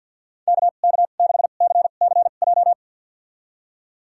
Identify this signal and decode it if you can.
Morse code, heard as a keyed tone, about 53 wpm, spaced as for 37 wpm, 690 Hz. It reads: KK6CCJ
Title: USA Call Signs - 2x Character Spacing - 50wpm